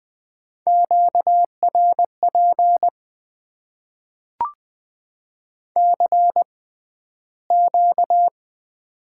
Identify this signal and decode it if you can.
Morse code, heard as a keyed tone, about 20 wpm, spaced as for 12 wpm, 700 Hz. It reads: QRP CQ
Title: Top 90 QSO Elements - 6x Character Spacing - 20wpm